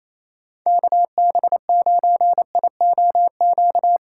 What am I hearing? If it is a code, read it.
Morse code, heard as a keyed tone, about 35 wpm, 700 Hz. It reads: KB9IOQ